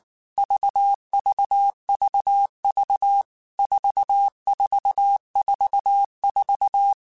Morse code, heard as a keyed tone, about 19 wpm, 800 Hz. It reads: VVVV 4444
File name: Basic Letter Groups By7Groups / VB46 / VB46@18